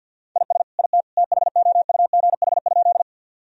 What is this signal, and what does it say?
Morse code, heard as a keyed tone, about 50 wpm, 700 Hz. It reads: IS IT THOUGH?
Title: Sentences from Top 1000 Words - Review Part 2 - 50wpm